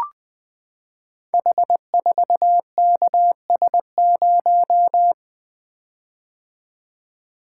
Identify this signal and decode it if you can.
Morse code, heard as a keyed tone, about 20 wpm, 700 Hz. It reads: H4KS0
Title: Code Groups - 5 Letter:Numbers - 20wpm